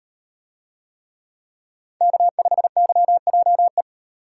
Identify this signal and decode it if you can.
Morse code, heard as a keyed tone, about 38 wpm, 700 Hz. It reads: K5YJE